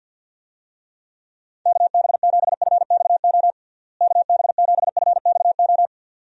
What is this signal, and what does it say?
Morse code, heard as a keyed tone, about 51 wpm, 700 Hz. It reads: KB7FXO KB7FXO
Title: USA Call Signs - 6 Characters - Repeated 2x - 50wpm